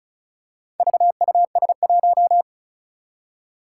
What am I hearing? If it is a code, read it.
Morse code, heard as a keyed tone, about 36 wpm, 700 Hz. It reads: VUS1
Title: Four Challenging Letter-Number Combinations 35wpm